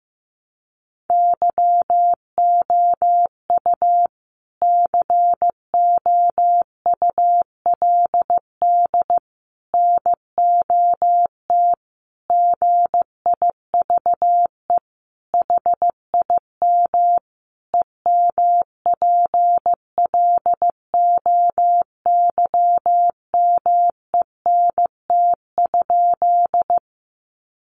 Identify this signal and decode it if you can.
Morse code, heard as a keyed tone, about 15 wpm, 700 Hz. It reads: YOU COULD NOT GIVE HIM EMPLOYMENT?
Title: Sentences from Top 1500 Words 15wpm